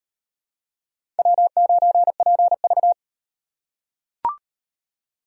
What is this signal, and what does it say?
Morse code, heard as a keyed tone, about 38 wpm, 700 Hz. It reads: W9PV